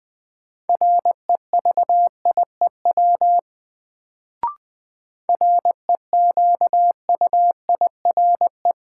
Morse code, heard as a keyed tone, about 20 wpm, 700 Hz. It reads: REVIEW REQUIRE
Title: Sentences from Top 900 Words - New Word Drill - 20wpm